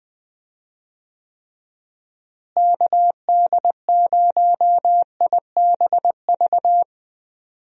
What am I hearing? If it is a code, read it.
Morse code, heard as a keyed tone, about 20 wpm, 700 Hz. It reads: KD0IBV
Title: USA Call Signs - 6 Characters - Repeated 2x - 20wpm